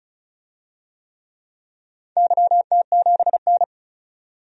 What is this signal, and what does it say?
Morse code, heard as a keyed tone, about 36 wpm, 700 Hz. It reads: YT7N
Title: DXCC-100 Call Signs - 4 Characters - 35wpm